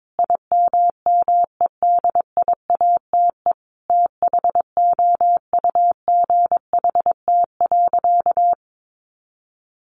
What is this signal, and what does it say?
Morse code, heard as a keyed tone, about 22 wpm, 700 Hz. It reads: IMMEDIATE THOUGHT.
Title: Sentences from Top 1500 Words 22wpm